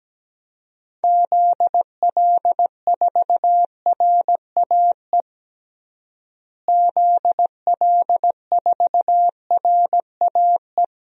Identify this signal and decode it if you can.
Morse code, heard as a keyed tone, about 17 wpm, 700 Hz. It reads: ZL4RAE ZL4RAE
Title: DXCC-100 Call Signs - 6 Characters - Repeated 3x - 17wpm